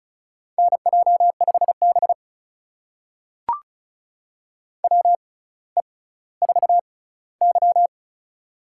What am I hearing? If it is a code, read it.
Morse code, heard as a keyed tone, about 36 wpm, spaced as for 18 wpm, 700 Hz. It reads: NJ5B WE4Y
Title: USA Call Signs - 4 Characters - 6x Character Spacing - 35wpm